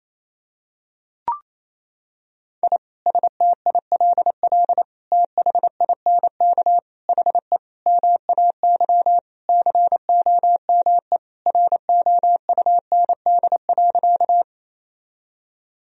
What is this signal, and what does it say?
Morse code, heard as a keyed tone, about 28 wpm, 700 Hz. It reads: I STILL THINK HE MAY COME ROUND.
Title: Sentences from Top 1000 Words - Review Part 1 - 28wpm